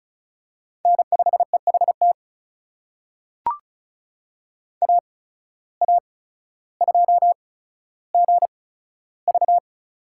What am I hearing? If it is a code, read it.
Morse code, heard as a keyed tone, about 34 wpm, spaced as for 15 wpm, 700 Hz. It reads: N5EHT AA2GV
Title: USA Call Signs - 5 Characters - 8x Character Spacing - 35wpm